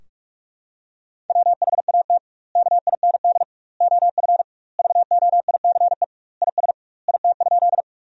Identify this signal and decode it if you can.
Morse code, heard as a keyed tone, about 41 wpm, 700 Hz. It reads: WHAT KIND OF VOICE IS IT?
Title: 2x Word-Spacing - Sentences from Top 500 Words - 45wpm